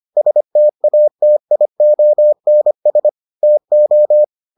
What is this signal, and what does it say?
Morse code, heard as a keyed tone, about 25 wpm, 600 Hz. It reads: STATIONS TO